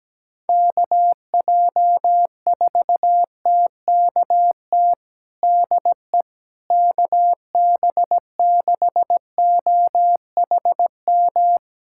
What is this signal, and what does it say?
Morse code, heard as a keyed tone, about 17 wpm, 700 Hz. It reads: KJ4TKT DE KB6OHM
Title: USA Callsign DE Callsign 17wpm